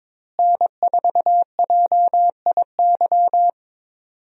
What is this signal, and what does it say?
Morse code, heard as a keyed tone, about 23 wpm, 700 Hz. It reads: N4JIY
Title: USA Call Signs - 5 Characters - Repeated 3x - 22wpm